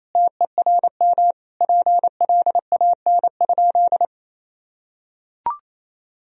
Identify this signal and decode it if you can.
Morse code, heard as a keyed tone, about 28 wpm, 700 Hz. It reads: TERM PLAN?
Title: Sentences from Top 1000 Words - Review Part 4 - 28wpm